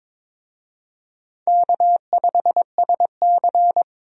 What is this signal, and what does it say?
Morse code, heard as a keyed tone, about 21 wpm, 700 Hz. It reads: K5SC
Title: USA Call Signs - 4 Characters - 22wpm